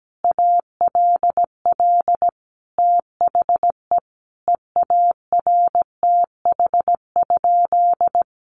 Morse code, heard as a keyed tone, about 17 wpm, 700 Hz. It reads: ALL THE EARTH?
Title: Sentences from Top 1500 Words 17wpm